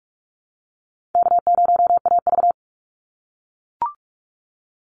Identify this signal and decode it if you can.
Morse code, heard as a keyed tone, about 44 wpm, 700 Hz. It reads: K0AV
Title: Easy Call Signs - 45wpm